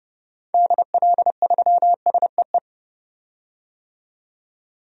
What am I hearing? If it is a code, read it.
Morse code, heard as a keyed tone, about 31 wpm, 700 Hz. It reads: DL3SEE
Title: DXCC-100 Call Signs - 6 Characters - 30wpm